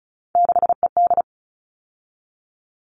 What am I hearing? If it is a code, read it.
Morse code, heard as a keyed tone, about 34 wpm, 700 Hz. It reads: 6ED
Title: All Cut Numbers - 3-Digit 35wpm